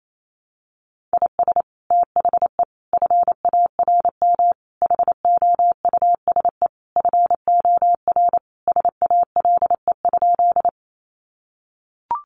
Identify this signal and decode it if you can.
Morse code, heard as a keyed tone, about 28 wpm, 700 Hz. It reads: IS THE FARM HOUSE FOR SALE?
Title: Sentences from Top 1400 Words 28wpm